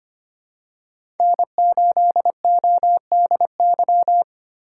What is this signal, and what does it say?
Morse code, heard as a keyed tone, about 25 wpm, 700 Hz. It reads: N8ODY
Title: USA Call Signs - 5 Characters - 2x Character Spacing - 25wpm